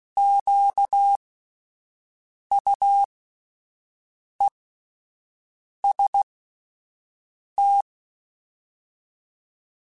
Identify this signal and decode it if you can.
Morse code, heard as a keyed tone, about 16 wpm, spaced as for 5 wpm, 800 Hz. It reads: QUEST
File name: Alphabetical / Alphabetical mp3s / Qs @5